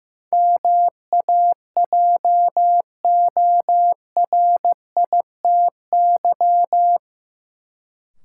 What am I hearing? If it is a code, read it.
Morse code, heard as a keyed tone, about 15 wpm, 700 Hz. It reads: MAJORITY
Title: Sets of 3 Words - New Word Drill - 15wpm